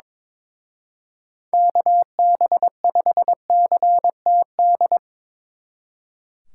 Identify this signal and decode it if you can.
Morse code, heard as a keyed tone, about 24 wpm, 700 Hz. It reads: KB5CTD
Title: Speed Racing - USA 2x3 Call Signs 22wpm